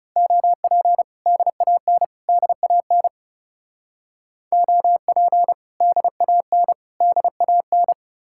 Morse code, heard as a keyed tone, about 36 wpm, 700 Hz. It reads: OP DAN DAN OP DAN DAN
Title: Speed-Racing Common QSO Phrases - 30wpm